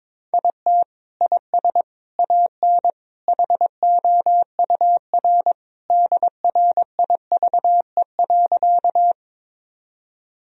Morse code, heard as a keyed tone, about 22 wpm, 700 Hz. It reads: IT IS AN HOUR DRIVE.
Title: Sentences from Top 1000 Words - Review Part 2 - 22wpm